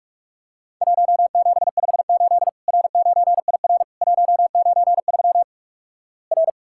Morse code, heard as a keyed tone, about 48 wpm, 700 Hz. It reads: 1758 R9IR 193 R
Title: Worked All Europe DX Contest - QTC Exchange - 45wpm